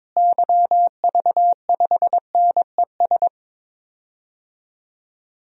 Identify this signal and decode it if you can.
Morse code, heard as a keyed tone, about 22 wpm, 700 Hz. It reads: YV5NES
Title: DXCC-100 Call Signs - 6 Characters - Repeated 2x - 22wpm